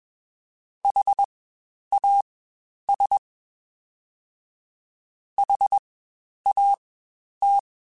Morse code, heard as a keyed tone, about 21 wpm, spaced as for 10 wpm, 800 Hz. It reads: HAS HAT
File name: Spaced Alphas / Spaced Alphas mp3s / Hs sp @20